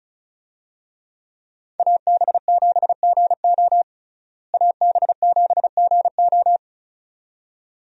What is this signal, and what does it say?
Morse code, heard as a keyed tone, about 34 wpm, 700 Hz. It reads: AB7GO AB7GO
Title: USA Call Signs - 5 Characters - Repeated 2x - 35wpm